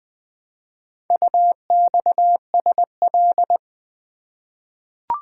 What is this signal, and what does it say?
Morse code, heard as a keyed tone, about 20 wpm, 700 Hz. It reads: UXSL
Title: Four Letter-Number Combinations - 20wpm